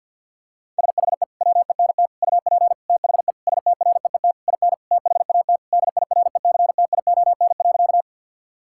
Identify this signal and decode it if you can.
Morse code, heard as a keyed tone, about 50 wpm, 700 Hz. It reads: SHE WENT UP THE STREET IN THAT DIRECTION.
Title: Sentences from Top 1000 Words - Review Part 5 - 50wpm